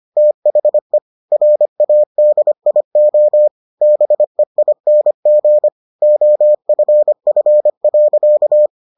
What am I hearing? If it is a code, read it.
Morse code, heard as a keyed tone, about 25 wpm, 600 Hz. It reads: THE RADIO BEING OFF.